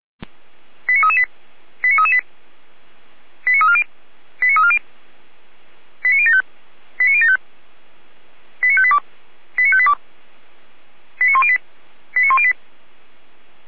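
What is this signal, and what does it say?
Five-tone selective call (ZVEI), heard as quick runs of five tones, tones of about 80 ms. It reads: 89338 89338 89340 89340 89075 89075 89582 89582 89108 89108
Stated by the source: ZVEI five-tone call 89330 received by SDR.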